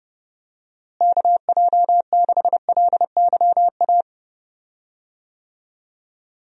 Morse code, heard as a keyed tone, about 32 wpm, 700 Hz. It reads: KJ6LYA
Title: USA Call Signs - 6 Characters - Repeated 2x - 30wpm